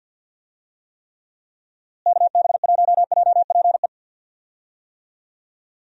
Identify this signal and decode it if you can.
Morse code, heard as a keyed tone, about 48 wpm, 700 Hz. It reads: KB1JPE
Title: USA Call Signs - 6 Characters - Repeated 3x - 50wpm